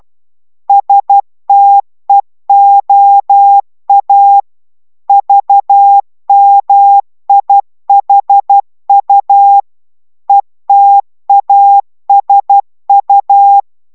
Morse code, heard as a keyed tone, about 12 wpm, 800 Hz. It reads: STEOA VMIHU ETASU
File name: taviro tanulas / cw 69 05 10